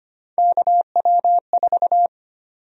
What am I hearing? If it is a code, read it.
Morse code, heard as a keyed tone, about 25 wpm, 700 Hz. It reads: KW4
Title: USA Call Sign Prefixes - 25wpm